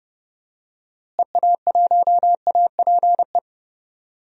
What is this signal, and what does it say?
Morse code, heard as a keyed tone, about 30 wpm, 700 Hz. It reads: EA1APE
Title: DXCC-100 Call Signs - 6 Characters - Repeated 2x - 30wpm